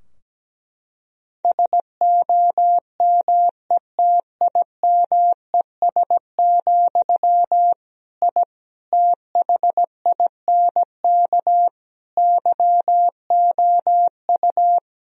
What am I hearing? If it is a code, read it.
Morse code, heard as a keyed tone, about 17 wpm, 700 Hz. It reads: SOMETIMES, I THINK YOU
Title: Sentences from Top 1000 Words - Review Part 3 - 17wpm